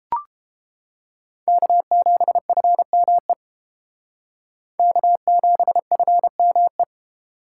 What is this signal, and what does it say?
Morse code, heard as a keyed tone, about 28 wpm, 700 Hz. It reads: K7FME K7FME